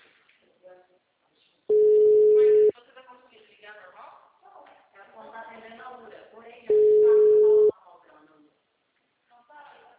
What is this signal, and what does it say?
Ringback tone (Europe, 425 Hz), with speech.
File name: out- -2016- - - .25815